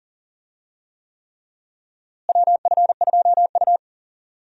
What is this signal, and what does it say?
Morse code, heard as a keyed tone, about 40 wpm, 700 Hz. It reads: WF2U
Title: USA Call Signs - 4 Characters - Repeated 3x - 40wpm